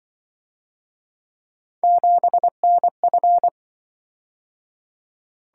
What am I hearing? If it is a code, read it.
Morse code, heard as a keyed tone, about 25 wpm, 700 Hz. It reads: 7NF
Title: Speed-Racing - Three Letter-Number Combinations - 20wpm